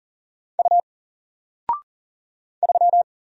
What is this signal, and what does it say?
Morse code, heard as a keyed tone, about 39 wpm, 700 Hz. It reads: U 3
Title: Single Letter-Number 40wpm